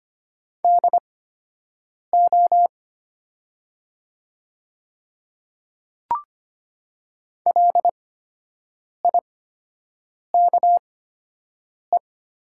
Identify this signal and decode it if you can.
Morse code, heard as a keyed tone, about 25 wpm, spaced as for 7 wpm, 700 Hz. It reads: DO LIKE
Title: 8x Character Spacing - Top 100 Words - 25wpm